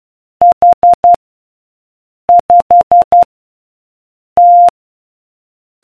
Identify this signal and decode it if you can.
Morse code, heard as a keyed tone, about 12 wpm, 700 Hz. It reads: H 5 T